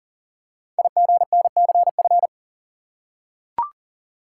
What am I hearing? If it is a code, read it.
Morse code, heard as a keyed tone, about 42 wpm, 700 Hz. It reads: IGNCF
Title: Code Groups - 5 Letters - 40wpm